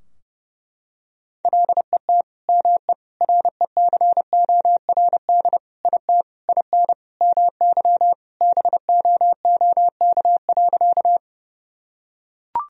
Morse code, heard as a keyed tone, about 30 wpm, 700 Hz. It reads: LET ME RECORD IT IN MY BOOK.
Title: Sentences from Top 1000 Words - Review Part 3 - 30wpm